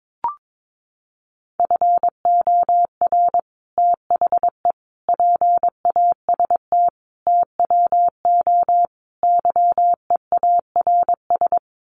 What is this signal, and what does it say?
Morse code, heard as a keyed tone, about 22 wpm, 700 Hz. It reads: FOR THE PAST TWO YEARS
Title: Sets of 5 Words Encore2 22wpm